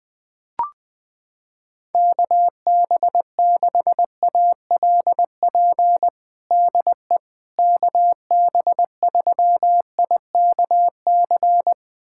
Morse code, heard as a keyed tone, about 20 wpm, 700 Hz. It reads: KB6ALP DE KB3IKC
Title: USA Callsign DE Callsign 20wpm